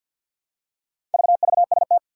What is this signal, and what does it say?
Morse code, heard as a keyed tone, about 47 wpm, 700 Hz. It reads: VVST
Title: Four Challenging Letter-Number Combinations 50wpm